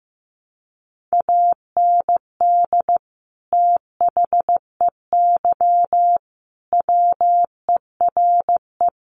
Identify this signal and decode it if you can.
Morse code, heard as a keyed tone, about 15 wpm, 700 Hz. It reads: AND THEY WERE
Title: Sentences from Top 1300 Words 15wpm